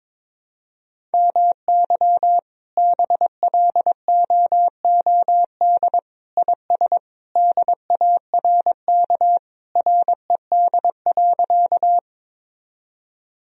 Morse code, heard as a keyed tone, about 22 wpm, 700 Hz. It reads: MY BLOOD IS DARK RED.
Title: Sentences from Top 1000 Words - Review Part 2 - 22wpm